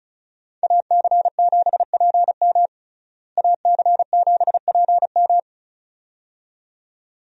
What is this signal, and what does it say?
Morse code, heard as a keyed tone, about 36 wpm, 700 Hz. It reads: AC7PM AC7PM
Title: USA Call Signs - 5 Characters - Repeated 2x - 35wpm